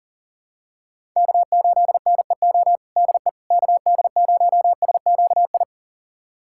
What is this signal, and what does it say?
Morse code, heard as a keyed tone, about 40 wpm, 700 Hz. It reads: K8NEO DE KD0SQI
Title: USA Callsign DE Callsign 40wpm